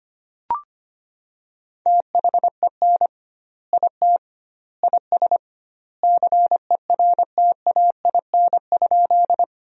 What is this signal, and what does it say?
Morse code, heard as a keyed tone, about 25 wpm, 700 Hz. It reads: THEN IT IS CERTAIN?
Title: 2x Word-Spacing - Sentences from Top 500 Words - 25wpm